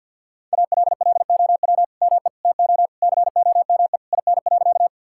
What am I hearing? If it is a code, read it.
Morse code, heard as a keyed tone, about 52 wpm, 700 Hz. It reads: ALLOW ME TO COME IN.
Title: Sentences from Top 1000 Words - Review Part 5 - 50wpm